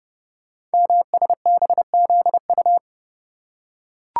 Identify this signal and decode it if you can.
Morse code, heard as a keyed tone, about 30 wpm, 690 Hz. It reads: MSBZU
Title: Code Groups - 5 Letters - 30wpm